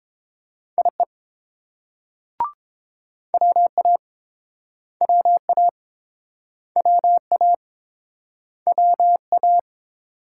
Morse code, heard as a keyed tone, about 32 wpm, 700 Hz. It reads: IE WA WA WA WA